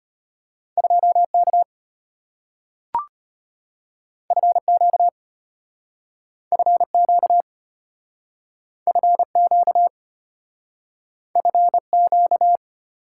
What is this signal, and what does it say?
Morse code, heard as a keyed tone, about 34 wpm, 690 Hz. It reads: 2K FQ FQ FQ FQ